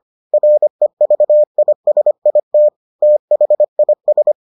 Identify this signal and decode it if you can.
Morse code, heard as a keyed tone, about 25 wpm, 600 Hz. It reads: REVISIT THIS